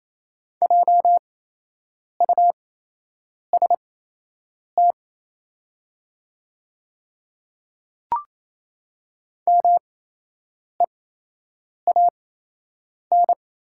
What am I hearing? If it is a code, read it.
Morse code, heard as a keyed tone, about 28 wpm, spaced as for 8 wpm, 700 Hz. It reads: JUST MEAN
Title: 8x Character Spacing - Top 200 Words - 28wpm